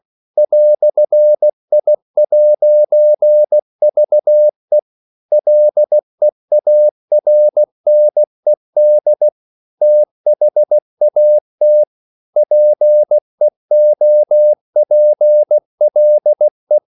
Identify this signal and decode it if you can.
Morse code, heard as a keyed tone, about 16 wpm, 600 Hz. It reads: "I'VE LEARNED THAT PEOPLE
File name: CW / Practice / EnglishQuotations0216